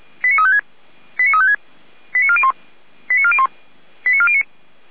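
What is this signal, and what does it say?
Five-tone selective call (ZVEI), heard as quick runs of five tones, tones of about 70 ms. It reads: 89356 89356 89401 89401 89409
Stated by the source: ZVEI five-tone call 89104 received on SDR, from Feuerwehr Orlamünde.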